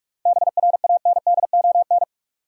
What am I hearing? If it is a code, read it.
Morse code, heard as a keyed tone, about 46 wpm, 700 Hz. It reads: BRANDON
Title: USA Common Names - 45wpm